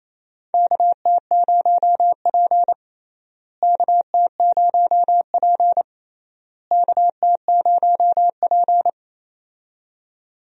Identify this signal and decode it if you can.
Morse code, heard as a keyed tone, about 28 wpm, 700 Hz. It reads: KT0P KT0P KT0P
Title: USA Call Signs - 4 Characters - Repeated 3x - 28wpm